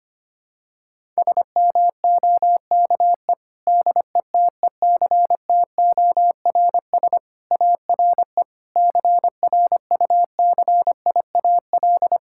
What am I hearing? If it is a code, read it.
Morse code, heard as a keyed tone, about 25 wpm, 700 Hz. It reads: SMOKE DETECTORS ARE CRUCIAL
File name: practice / 25wpm / Difficult Sentences with Hints - 25wpm